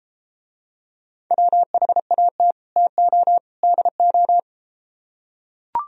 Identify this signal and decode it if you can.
Morse code, heard as a keyed tone, about 33 wpm, 700 Hz. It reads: WHAT TO DO